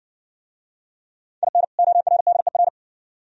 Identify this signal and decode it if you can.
Morse code, heard as a keyed tone, about 58 wpm, 700 Hz. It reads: IN ORDER